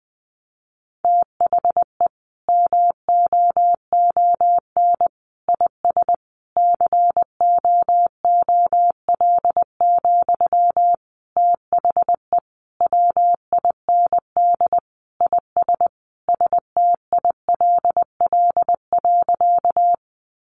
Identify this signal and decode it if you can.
Morse code, heard as a keyed tone, about 20 wpm, 700 Hz. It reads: THE MOON IS COOL, THE WIND IS STILL.
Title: Sentences from Top 1200 Words 20wpm